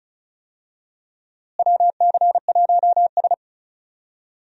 Morse code, heard as a keyed tone, about 34 wpm, 700 Hz. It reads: WC1S
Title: USA Call Signs - 4 Characters - Repeated 3x - 35wpm